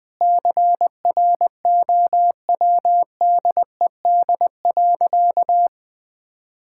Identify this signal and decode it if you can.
Morse code, heard as a keyed tone, about 20 wpm, 700 Hz. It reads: CROWDED.
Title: Sentences with 7-Word Phrases - 20wpm